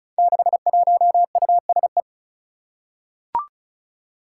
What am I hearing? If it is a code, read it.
Morse code, heard as a keyed tone, about 36 wpm, 700 Hz. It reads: 61USE
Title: Five Challenging Letter-Number Combinations 35wpm